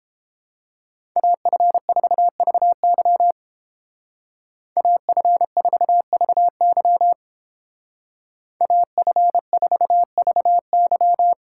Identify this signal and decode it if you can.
Morse code, heard as a keyed tone, about 31 wpm, 700 Hz. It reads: AF4VY AF4VY AF4VY